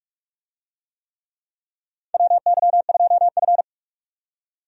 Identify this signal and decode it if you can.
Morse code, heard as a keyed tone, about 46 wpm, 700 Hz. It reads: WY2F
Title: USA Call Signs - 4 Characters - 45wpm